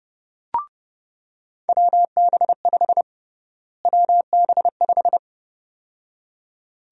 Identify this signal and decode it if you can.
Morse code, heard as a keyed tone, about 30 wpm, 700 Hz. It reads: WB5 WB5
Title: USA Call Sign Prefixes - Repeated 2x - 30wpm